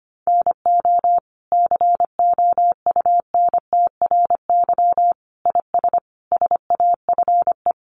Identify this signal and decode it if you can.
Morse code, heard as a keyed tone, about 25 wpm, 700 Hz. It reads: NO COUNTRY IS SAFE
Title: Sentences from Top 900 Words 25wpm